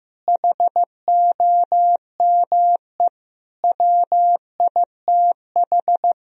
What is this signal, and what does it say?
Morse code, heard as a keyed tone, about 15 wpm, 700 Hz. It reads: HOME WITH
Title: Sentences from Top 1000 Words - Review Part 1 - 15wpm